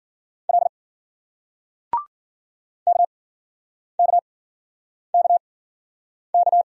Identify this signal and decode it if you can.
Morse code, heard as a keyed tone, about 52 wpm, spaced as for 9 wpm, 700 Hz. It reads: L KKKK